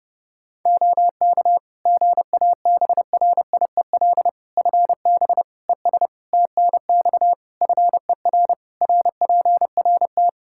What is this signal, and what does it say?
Morse code, heard as a keyed tone, about 30 wpm, 700 Hz. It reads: OK GABRIEL FB ES TNX FER RPRT
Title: Practice QSO - Exchange 2 - Info - 30wpm